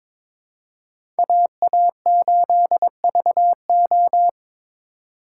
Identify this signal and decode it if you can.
Morse code, heard as a keyed tone, about 22 wpm, 700 Hz. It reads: AA8VO
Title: USA Call Signs - 5 Characters - 3x Character Spacing - 22wpm